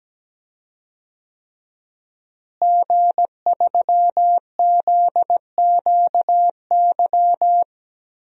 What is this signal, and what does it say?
Morse code, heard as a keyed tone, about 17 wpm, 700 Hz. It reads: G3ZQY
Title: DXCC-100 Call Signs - 5 Characters - Repeated 3x - 17wpm